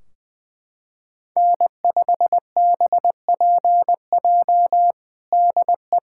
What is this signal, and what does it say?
Morse code, heard as a keyed tone, about 21 wpm, 700 Hz. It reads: N5BPJ DE
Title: K1USN SST QSO Practice 20wpm